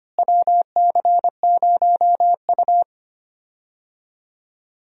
Morse code, heard as a keyed tone, about 25 wpm, 700 Hz. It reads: WC0U
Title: USA Call Signs - 4 Characters - Repeated 2x - 25wpm